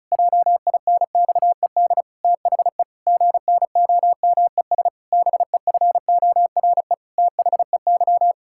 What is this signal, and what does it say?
Morse code, heard as a keyed tone, about 34 wpm, 700 Hz. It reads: JINXED THE GNOMES BEFORE THEY
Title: Pangram Challenge v2 - 35wpm